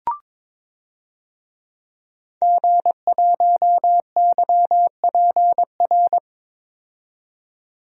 Morse code, heard as a keyed tone, about 22 wpm, 700 Hz. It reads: G1YPR
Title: DXCC-100 Call Signs - 5 Characters - 22wpm